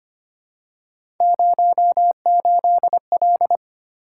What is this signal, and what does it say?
Morse code, heard as a keyed tone, about 25 wpm, 700 Hz. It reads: 08L
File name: Three Letter-Number Combinations - 25wpm